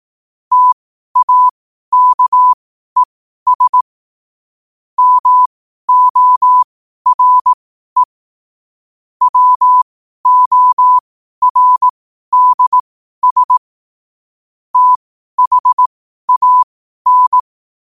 Morse code, heard as a keyed tone, about 18 wpm, spaced as for 12 wpm, 1000 Hz. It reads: TAKES MORE WORDS THAN
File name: qotdfiles / 12wpm